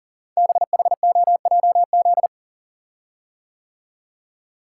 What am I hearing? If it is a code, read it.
Morse code, heard as a keyed tone, about 39 wpm, 700 Hz. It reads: BHOJZ
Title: Code Groups - 5 Letters - 40wpm